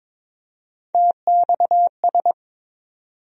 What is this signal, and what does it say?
Morse code, heard as a keyed tone, about 21 wpm, 700 Hz. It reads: TXS
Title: USA Call Sign Postfixes - 3 Letters 22wpm